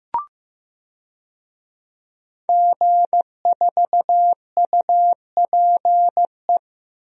Morse code, heard as a keyed tone, about 15 wpm, 700 Hz. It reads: G4UPE
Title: DXCC-100 Call Signs - 5 Characters - 15wpm